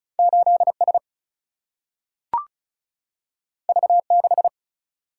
Morse code, heard as a keyed tone, about 36 wpm, 700 Hz. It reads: 8 S V 6
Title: Single-Letter Single-Number 35wpm